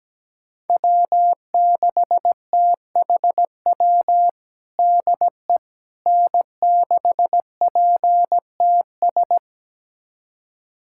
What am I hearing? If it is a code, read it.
Morse code, heard as a keyed tone, about 17 wpm, 700 Hz. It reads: W6THW DE N6PTS
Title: USA Callsign DE Callsign - 5 Characters - 17wpm